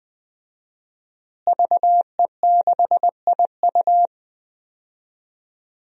Morse code, heard as a keyed tone, about 20 wpm, 700 Hz. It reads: VE6IU
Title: DXCC-100 Call Signs - 5 Characters - Repeated 3x - 20wpm